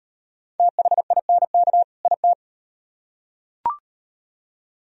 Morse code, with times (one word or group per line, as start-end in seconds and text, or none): 0.6-1.8 THINK
2.1-2.3 IT